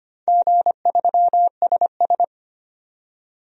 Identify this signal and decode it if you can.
Morse code, heard as a keyed tone, about 25 wpm, 700 Hz. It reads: G3SS
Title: DXCC-100 Call Signs - 4 Characters - 25wpm